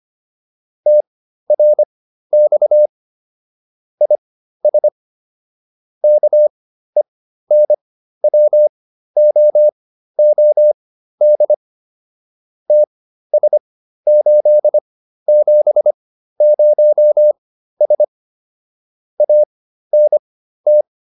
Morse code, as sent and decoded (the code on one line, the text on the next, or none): - .-. -..- / .. ... / -.- . -. .-- --- --- -.. / - ... ---.. --... ----- ... / .- -. -
TRX IS KENWOOD TS870S ANT